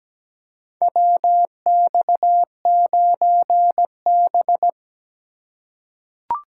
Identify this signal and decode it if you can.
Morse code, heard as a keyed tone, about 17 wpm, 700 Hz. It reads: WX9B
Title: USA Call Signs - 4 Characters - 3x Character Spacing - 17wpm